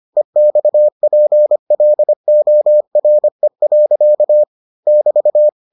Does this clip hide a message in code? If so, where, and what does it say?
Yes: Morse code from 0.2 to 5.5 s, heard as a keyed tone, about 25 wpm, 600 Hz. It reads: EXPLORE. <BT>